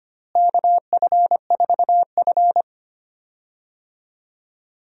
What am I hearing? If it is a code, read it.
Morse code, heard as a keyed tone, about 25 wpm, 700 Hz. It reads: KF4F
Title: USA Call Signs - 4 Characters - Repeated 3x - 25wpm